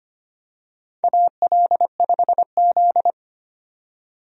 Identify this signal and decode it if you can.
Morse code, heard as a keyed tone, about 25 wpm, 700 Hz. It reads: AL5Z
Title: USA Call Signs - 4 Characters - 25wpm